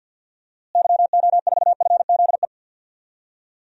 Morse code, heard as a keyed tone, about 49 wpm, 700 Hz. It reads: YO4FZE
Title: DXCC-100 Call Signs - 6 Characters - Repeated 2x - 50wpm